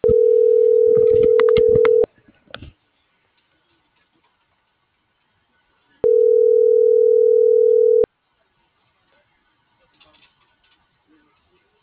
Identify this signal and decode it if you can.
Ringback tone (North America), 440 + 480 Hz.
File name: CENTRAL GENERAL / out rubyalejandra